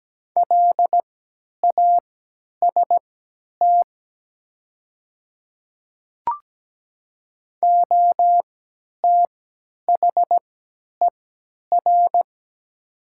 Morse code, heard as a keyed tone, about 17 wpm, spaced as for 10 wpm, 700 Hz. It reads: LAST OTHER
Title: 3x Character Spacing - Top 300 Words - 17wpm